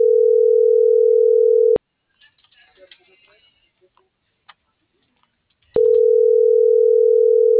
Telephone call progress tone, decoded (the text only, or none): ringback tone